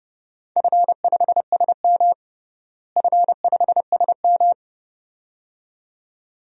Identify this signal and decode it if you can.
Morse code, heard as a keyed tone, about 30 wpm, 700 Hz. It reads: F5SM F5SM
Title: DXCC-100 Call Signs - 4 Characters - Repeated 2x - 30wpm